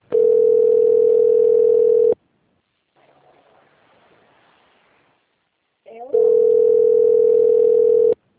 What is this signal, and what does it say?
Ringback tone (North America), 440 + 480 Hz.
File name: rg-1600-unknown- - - .1542